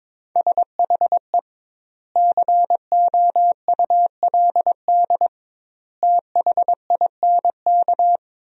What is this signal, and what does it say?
Morse code, heard as a keyed tone, about 22 wpm, 700 Hz. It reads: SHE COULD THINK
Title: 2x Word-Spacing - Sentences from Top 500 Words - 22wpm